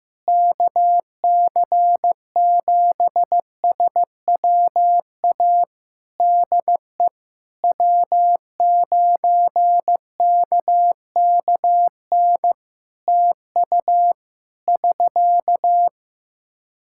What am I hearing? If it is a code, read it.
Morse code, heard as a keyed tone, about 15 wpm, 700 Hz. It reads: KC7SWA DE W9KKN TU <SK>
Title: Practice QSO - Exchange 4 - Ending - 15wpm